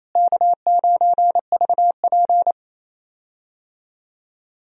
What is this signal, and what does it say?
Morse code, heard as a keyed tone, about 27 wpm, 700 Hz. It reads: K9VP
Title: USA Call Signs - 4 Characters - Repeated 2x - 28wpm